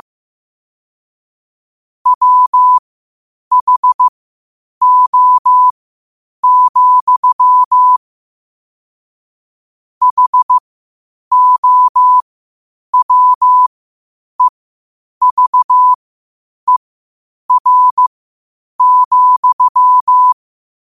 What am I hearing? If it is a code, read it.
Morse code, heard as a keyed tone, about 15 wpm, spaced as for 8 wpm, 1000 Hz. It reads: WHO, HOWEVER,
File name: qotdfiles / 8wpm